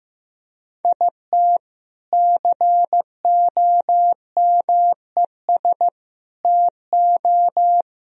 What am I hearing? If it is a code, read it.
Morse code, heard as a keyed tone, about 17 wpm, 700 Hz. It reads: IT COMES TO
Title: Speed-Racing - Sets of 3 Words 15wpm